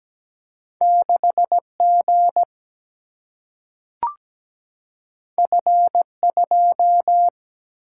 Morse code, heard as a keyed tone, about 17 wpm, 700 Hz. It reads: 6G F2
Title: Single-Letter Single-Number 17wpm